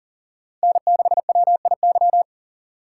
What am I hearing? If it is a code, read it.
Morse code, heard as a keyed tone, about 40 wpm, 700 Hz. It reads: N6WIY
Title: USA Call Signs - 4x Character Spacing - 40wpm